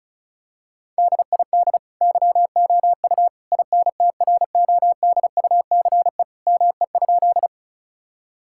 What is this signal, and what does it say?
Morse code, heard as a keyed tone, about 34 wpm, 700 Hz. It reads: DID YOU INTRODUCE ME?
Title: Sentences from Top 1600 Words 35wpm